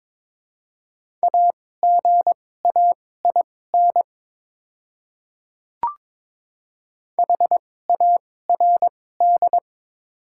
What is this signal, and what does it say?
Morse code, heard as a keyed tone, about 22 wpm, spaced as for 16 wpm, 700 Hz. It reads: AGAIN HARD
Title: 2x Character Spacing - Top 300 Words - 22wpm